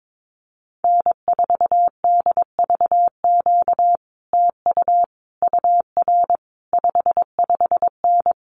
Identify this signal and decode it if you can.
Morse code, heard as a keyed tone, about 22 wpm, 700 Hz. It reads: N4DVQ TU UR 55N
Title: Common QSO Phrases Encore 22wpm